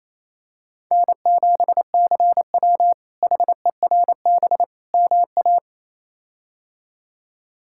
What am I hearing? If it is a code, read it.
Morse code, heard as a keyed tone, about 27 wpm, 700 Hz. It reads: N7CW HERB MA
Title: K1USN SST QSO Practice - 28wpm